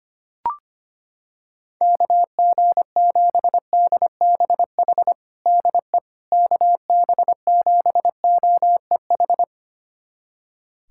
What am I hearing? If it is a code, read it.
Morse code, heard as a keyed tone, about 25 wpm, 700 Hz. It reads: KG7DBH DE KB7OEH
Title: USA Callsign DE Callsign - 6 Characters - 25wpm